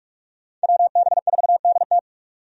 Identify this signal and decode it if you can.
Morse code, heard as a keyed tone, about 46 wpm, 700 Hz. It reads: WB4DT
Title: USA Call Signs - 5 Characters - 45wpm